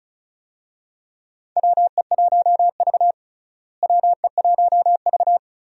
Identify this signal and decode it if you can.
Morse code, heard as a keyed tone, about 34 wpm, 700 Hz. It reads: WE1V WE1V
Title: USA Call Signs - 4 Characters - Repeated 3x - 35wpm